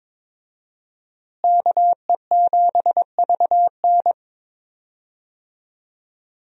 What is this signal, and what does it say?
Morse code, heard as a keyed tone, about 22 wpm, 700 Hz. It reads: KE7VN
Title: USA Call Signs - 5 Characters - Repeated 2x - 22wpm